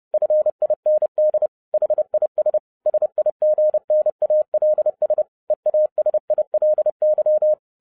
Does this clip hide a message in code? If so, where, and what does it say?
Yes: Morse code from 0.1 to 7.5 s, heard as a keyed tone, about 30 wpm, 610 Hz. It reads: FIND HIS SIGNALS EASILY